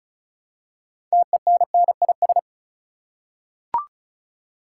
Morse code, heard as a keyed tone, about 36 wpm, 700 Hz. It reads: TENNIS
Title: Six Letter Words - 35wpm